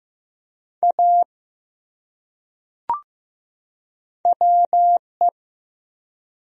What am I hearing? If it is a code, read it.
Morse code, heard as a keyed tone, about 15 wpm, 700 Hz. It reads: A WE
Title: Top 100 Words 15wpm